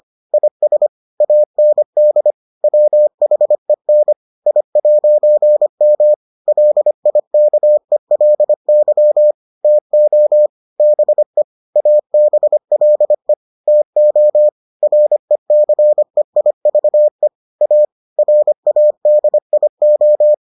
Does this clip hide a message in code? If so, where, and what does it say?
Yes: Morse code from 0.3 to 20.3 s, heard as a keyed tone, about 25 wpm, 600 Hz. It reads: IS AND WHEN I'M LIKELY TO BE ABLE TO RECEIVE A RADIO